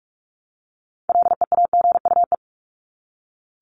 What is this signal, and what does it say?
Morse code, heard as a keyed tone, about 44 wpm, 700 Hz. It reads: LEAGUE
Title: Top 1751-2000 Words 45wpm